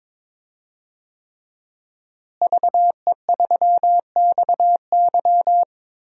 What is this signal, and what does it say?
Morse code, heard as a keyed tone, about 22 wpm, 700 Hz. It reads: VE3XY
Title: DXCC-100 Call Signs - 5 Characters - Repeated 3x - 22wpm